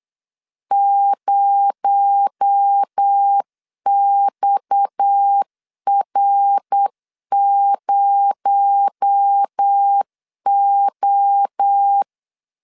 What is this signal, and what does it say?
Morse code, heard as a keyed tone, about 8 wpm, 800 Hz. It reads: 0XR0O
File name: CW / DET / 1 9 1500C S A 8wpm 5637